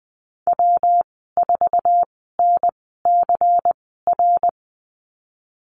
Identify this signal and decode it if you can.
Morse code, heard as a keyed tone, about 20 wpm, spaced as for 15 wpm, 700 Hz. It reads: W4NCR
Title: 154 Lesson 25g - Letter p - Call Signs 20wpm